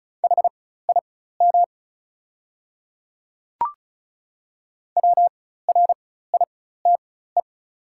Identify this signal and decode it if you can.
Morse code, heard as a keyed tone, about 36 wpm, spaced as for 16 wpm, 690 Hz. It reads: HIM WRITE
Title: 4x Character Spacing - Top 100 Words - 35wpm